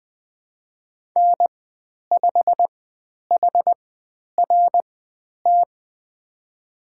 Morse code, heard as a keyed tone, about 20 wpm, spaced as for 10 wpm, 700 Hz. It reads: N5HRT
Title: 095 Lesson 16g - Letter c - Call Signs 20wpm